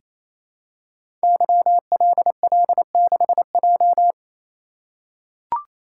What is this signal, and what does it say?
Morse code, heard as a keyed tone, about 28 wpm, 700 Hz. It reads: YLL6J
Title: Code Groups - 5 Letter:Numbers - 28wpm